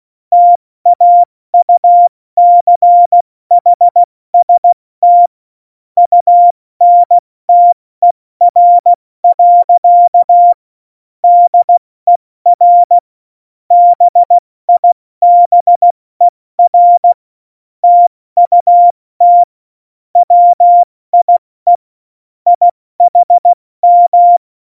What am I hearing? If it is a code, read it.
Morse code, heard as a keyed tone, about 16 wpm, spaced as for 14 wpm, 700 Hz. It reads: TAUCHST UNTER. DER BIBER TUT WIE IHM
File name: Morsekurs / Witze / Witz 08. Dezember 2020, 70 F80